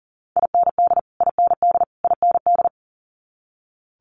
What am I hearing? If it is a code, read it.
Morse code, heard as a keyed tone, about 42 wpm, 700 Hz. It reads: IND IND IND
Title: Multi-letter Phonemes Repeated 3x 40wpm